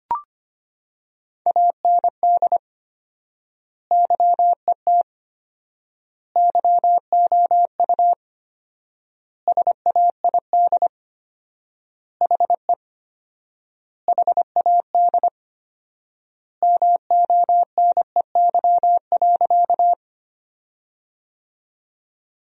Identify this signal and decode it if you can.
Morse code, heard as a keyed tone, about 25 wpm, 700 Hz. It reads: AND YET YOU SAID HE HAD MONEY.
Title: 4x Word-Spacing - Sentences from Top 500 Words - 25wpm